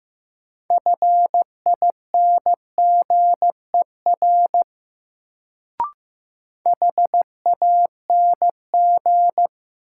Morse code, heard as a keyed tone, about 15 wpm, 700 Hz. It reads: FINGER HANG
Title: Sentences from Top 1600 Words - New Word Drill Alt - 15wpm